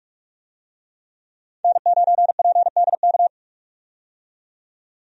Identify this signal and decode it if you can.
Morse code, heard as a keyed tone, about 42 wpm, 700 Hz. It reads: N9PDK
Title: USA Call Signs - 5 Characters - 45wpm